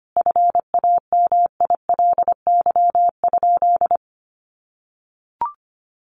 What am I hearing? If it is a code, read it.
Morse code, heard as a keyed tone, about 25 wpm, 700 Hz. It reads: FAMILY?
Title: Sentences from Top 1100 Words 25wpm